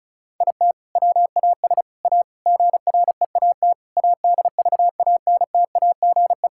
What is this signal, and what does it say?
Morse code, heard as a keyed tone, about 34 wpm, 700 Hz. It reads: IT WAS A GREAT ADVANTAGE
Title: Sentences from Top 1000 Words - Review Part 6 - 35wpm